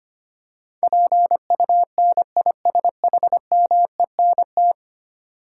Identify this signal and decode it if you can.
Morse code, heard as a keyed tone, about 25 wpm, 700 Hz. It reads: PUNISHMENT
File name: 10 Letter Words - First 3 Letters Given - 25wpm